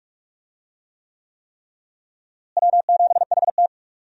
Morse code, heard as a keyed tone, about 46 wpm, 700 Hz. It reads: W7HT
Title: USA Call Signs - 4 Characters - Repeated 3x - 45wpm